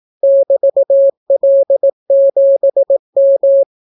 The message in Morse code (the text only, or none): <BT>L7M